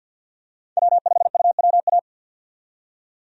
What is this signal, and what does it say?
Morse code, heard as a keyed tone, about 52 wpm, 700 Hz. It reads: W5UWA
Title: USA Call Signs - 5 Characters - 50wpm